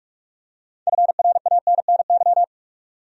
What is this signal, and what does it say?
Morse code, heard as a keyed tone, about 46 wpm, 700 Hz. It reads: FRANNY
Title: USA Common Names - 45wpm